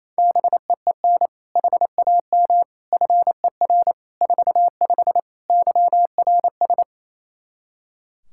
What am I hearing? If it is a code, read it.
Morse code, heard as a keyed tone, about 28 wpm, 700 Hz. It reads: BEEN HAM FER 45 YRS
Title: Practice QSO - Exchange 3 - More Info - 28wpm